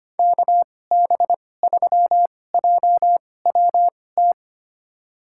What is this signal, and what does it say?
Morse code, heard as a keyed tone, about 25 wpm, spaced as for 18 wpm, 700 Hz. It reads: KB3JWT
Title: USA Call Signs - 6 Characters - 2x Character Spacing - 25wpm